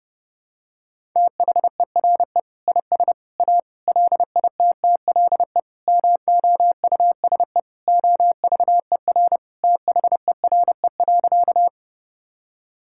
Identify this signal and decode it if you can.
Morse code, heard as a keyed tone, about 32 wpm, 700 Hz. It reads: THERE IS A LITTLE MOUSE OVER THERE.
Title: Sentences from Top 1000 Words - Review Part 5 - 30wpm